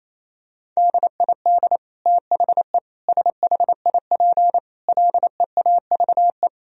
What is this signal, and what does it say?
Morse code, heard as a keyed tone, about 28 wpm, 700 Hz. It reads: DID THE SHIP LEAVE
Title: Sentences from Top 1000 Words - Review Part 2 - 28wpm